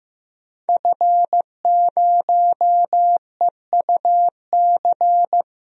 Morse code, heard as a keyed tone, about 15 wpm, 700 Hz. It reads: F0EUC
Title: DXCC-100 Call Signs - 5 Characters - 15wpm